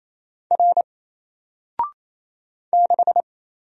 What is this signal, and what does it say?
Morse code, heard as a keyed tone, about 27 wpm, 700 Hz. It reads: R 6
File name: Single-Letter-Number-28wpm